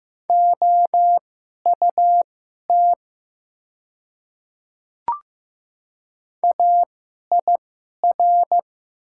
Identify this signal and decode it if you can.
Morse code, heard as a keyed tone, about 15 wpm, spaced as for 11 wpm, 700 Hz. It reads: OUT AIR
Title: 2x Character Spacing - Top 300 Words - 15wpm